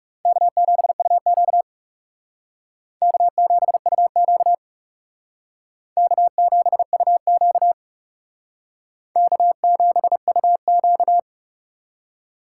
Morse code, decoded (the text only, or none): K7UQ K7UQ K7UQ K7UQ